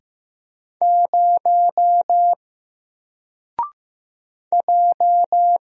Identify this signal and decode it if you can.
Morse code, heard as a keyed tone, about 15 wpm, 700 Hz. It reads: 0 J
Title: Single Letter-Number 15wpm